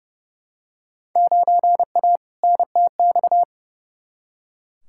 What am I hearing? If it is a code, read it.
Morse code, heard as a keyed tone, about 31 wpm, 700 Hz. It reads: 9A NTX
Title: Field Day Exchanges Encore - 30wpm